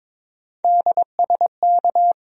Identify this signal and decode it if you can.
Morse code, heard as a keyed tone, about 22 wpm, 700 Hz. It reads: DSK
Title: USA Call Sign Postfixes - 3 Letters 22wpm